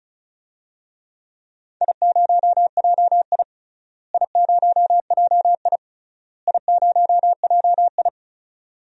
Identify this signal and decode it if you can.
Morse code, heard as a keyed tone, about 36 wpm, 690 Hz. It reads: I0JI I0JI I0JI
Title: DXCC-100 Call Signs - 4 Characters - Repeated 3x - 35wpm